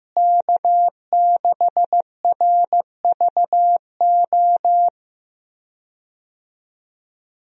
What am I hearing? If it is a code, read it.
Morse code, heard as a keyed tone, about 16 wpm, 700 Hz. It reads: K6RVO
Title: USA Call Signs - 5 Characters - Repeated 3x - 15wpm